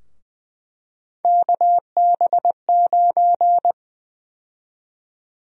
Morse code, heard as a keyed tone, about 21 wpm, 700 Hz. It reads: KB9
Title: USA Call Sign Prefixes - 20wpm